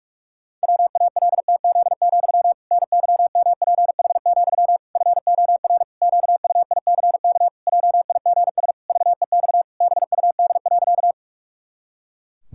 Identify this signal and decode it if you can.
Morse code, heard as a keyed tone, about 46 wpm, 700 Hz. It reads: WALTZ, NYMPH, FOR QUICK JIGS VEX BUD.
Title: Pangram Challenge v2 - 45wpm